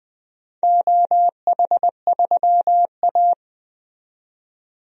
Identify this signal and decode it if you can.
Morse code, heard as a keyed tone, about 20 wpm, 700 Hz. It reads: OH3A
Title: DXCC-100 Call Signs - 4 Characters - 20wpm